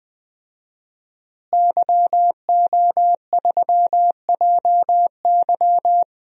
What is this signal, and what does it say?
Morse code, heard as a keyed tone, about 20 wpm, 700 Hz. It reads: YO3JY
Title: DXCC-100 Call Signs - 5 Characters - Repeated 3x - 20wpm